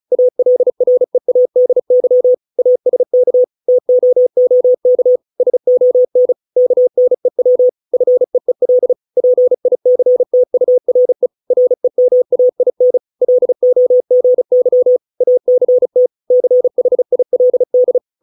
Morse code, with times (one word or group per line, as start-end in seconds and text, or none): 0.1-2.3 ALREADY
2.6-3.4 ASK
3.7-5.2 TOOK
5.4-6.3 SON
6.6-7.7 KNEW
7.9-8.9 FEEL
9.2-11.3 PICTURE
11.5-13.0 REMAIN
13.2-15.0 LOGY
15.2-16.1 ACT
16.3-18.0 CHILD